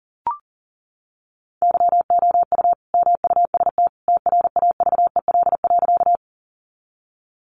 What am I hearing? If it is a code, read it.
Morse code, heard as a keyed tone, about 40 wpm, 700 Hz. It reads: YOU MUST TRAVEL.
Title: Sentences from Top 400 Words 40wpm